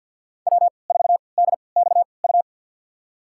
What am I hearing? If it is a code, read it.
Morse code, heard as a keyed tone, about 53 wpm, spaced as for 29 wpm, 690 Hz. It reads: W4DXU
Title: USA Call Signs - 5 Characters - 3x Character Spacing - 50wpm